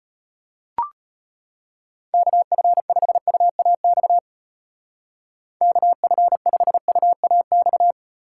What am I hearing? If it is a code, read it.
Morse code, heard as a keyed tone, about 33 wpm, 700 Hz. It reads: KF5UAX KF5UAX